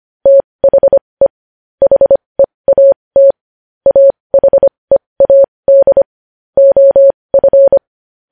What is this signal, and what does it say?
Morse code, heard as a keyed tone, about 25 wpm, spaced as for 20 wpm, 550 Hz. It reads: THE HEAT AHEAD OF
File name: podcast / News headlines-1x-25x20